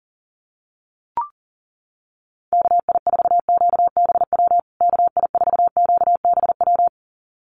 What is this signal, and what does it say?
Morse code, heard as a keyed tone, about 40 wpm, 700 Hz. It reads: KI4QBW KI4QBW
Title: USA Call Signs Repeated 2x 40wpm